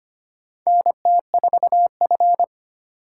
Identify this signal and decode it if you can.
Morse code, heard as a keyed tone, about 25 wpm, 700 Hz. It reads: NT4F
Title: USA Call Signs - 4 Characters - 2x Character Spacing - 25wpm